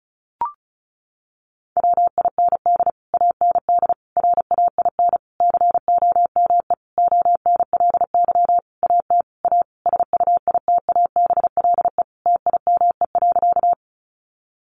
Morse code, heard as a keyed tone, about 34 wpm, 700 Hz. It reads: WIND AND RAIN COME ONLY AT A SUITABLE TIME.
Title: Sentences from Top 1400 Words 35wpm